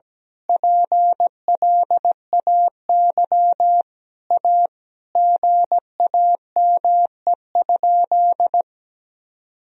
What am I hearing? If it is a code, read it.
Morse code, heard as a keyed tone, about 17 wpm, 700 Hz. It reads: PLAY A GAME?
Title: Sentences from Top 1000 Words - Review Part 1 - 17wpm